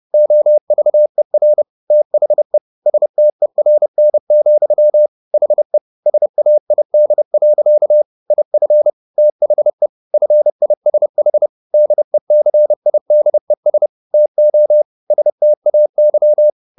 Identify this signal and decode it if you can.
Morse code, heard as a keyed tone, about 30 wpm, 600 Hz. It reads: OVER THE STERN, HE SAID. IF THE FISH DECIDES TO STAY
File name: the old man and the sea / the old man and the sea 1 30wpm